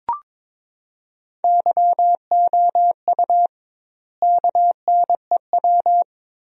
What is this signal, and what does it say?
Morse code, heard as a keyed tone, about 21 wpm, 700 Hz. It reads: YOU KNEW
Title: 2x Word-Spacing - Sentences from Top 500 Words - 22wpm